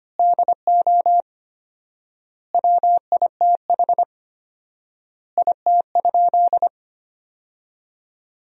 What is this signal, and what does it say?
Morse code, heard as a keyed tone, about 25 wpm, 700 Hz. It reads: DO WITH IT?
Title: 4x Word-Spacing - Sentences from Top 100 Words - 25wpm